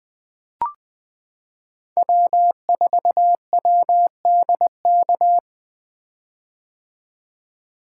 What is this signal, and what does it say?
Morse code, heard as a keyed tone, about 20 wpm, 700 Hz. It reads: W4WDK
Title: 2023 POTA Hunter Call Signs - 20wpm